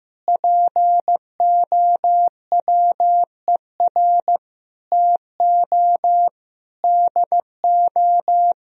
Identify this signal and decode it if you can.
Morse code, heard as a keyed tone, about 15 wpm, 700 Hz. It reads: POWER TO DO
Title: Sentences from Top 1000 Words - Review Part 3 - 15wpm